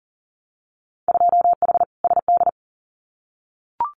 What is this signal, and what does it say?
Morse code, heard as a keyed tone, about 40 wpm, 700 Hz. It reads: 2H SD
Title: Winter Field Day Exchanges 40wpm